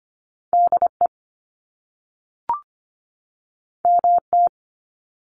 Morse code, heard as a keyed tone, about 25 wpm, 700 Hz. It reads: DE MT
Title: US States 25wpm